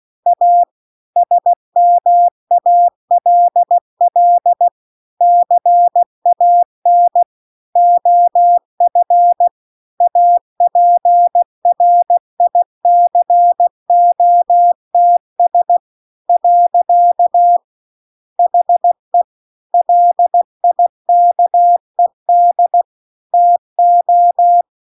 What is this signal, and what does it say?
Morse code, heard as a keyed tone, about 16 wpm, 700 Hz. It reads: A SMALL CAN OF APRICOTS . HE LIKED TO